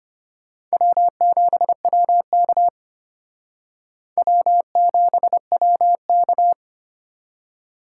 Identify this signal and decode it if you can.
Morse code, heard as a keyed tone, about 32 wpm, 700 Hz. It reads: W7WK W7WK